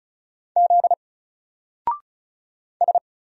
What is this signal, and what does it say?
Morse code, heard as a keyed tone, about 34 wpm, 700 Hz. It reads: Z S
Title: Single Character (Letter:Number:Punctuation:Prosign) 35wpm